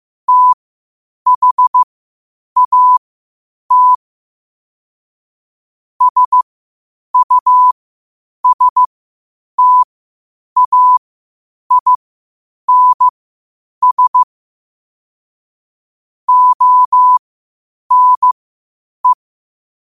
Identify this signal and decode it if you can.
Morse code, heard as a keyed tone, about 15 wpm, spaced as for 8 wpm, 1000 Hz. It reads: THAT SUSTAINS ONE
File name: qotdfiles / 8wpm